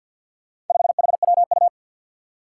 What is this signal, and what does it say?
Morse code, heard as a keyed tone, about 51 wpm, 700 Hz. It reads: 5HWU
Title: Four Challenging Letter-Number Combinations 50wpm